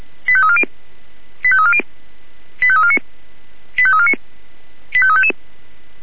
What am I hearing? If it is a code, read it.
Five-tone selective call (ZVEI), heard as quick runs of five tones, tones of about 70 ms. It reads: 85340 85340 85349 85349 85355